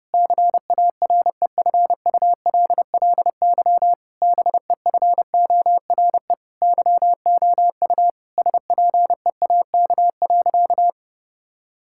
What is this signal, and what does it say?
Morse code, heard as a keyed tone, about 28 wpm, 700 Hz. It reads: CAREFULLY BEFORE YOU SPEAK.
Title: Sentences from Top 1600 Words 30wpm